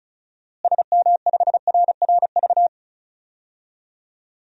Morse code, heard as a keyed tone, about 36 wpm, 700 Hz. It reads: SM5RRV
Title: DXCC-100 Call Signs - 6 Characters - Repeated 3x - 35wpm